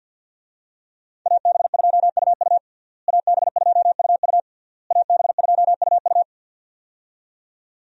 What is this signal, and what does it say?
Morse code, heard as a keyed tone, about 52 wpm, 690 Hz. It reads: AB2UU AB2UU AB2UU
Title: USA Call Signs - 5 Characters - Repeated 3x - 50wpm